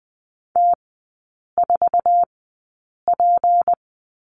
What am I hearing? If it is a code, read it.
Morse code, heard as a keyed tone, about 20 wpm, spaced as for 8 wpm, 690 Hz. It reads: T4P
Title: 159 Lesson 26e - Letter g - Three Characters 20wpm